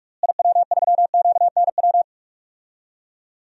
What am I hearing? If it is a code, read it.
Morse code, heard as a keyed tone, about 41 wpm, 700 Hz. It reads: IW3QNW
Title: DXCC-100 Call Signs - 6 Characters - Repeated 3x - 45wpm